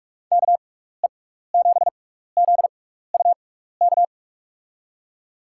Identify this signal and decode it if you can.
Morse code, heard as a keyed tone, about 46 wpm, spaced as for 16 wpm, 700 Hz. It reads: KE7ZUK
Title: USA Call Signs - 6 Characters - 6x Character Spacing - 45wpm